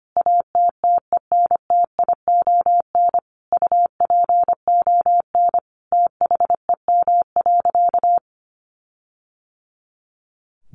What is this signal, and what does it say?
Morse code, heard as a keyed tone, about 25 wpm, 700 Hz. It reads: ATTENTION UPON THEM.
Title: Sentences from Top 1400 Words 25wpm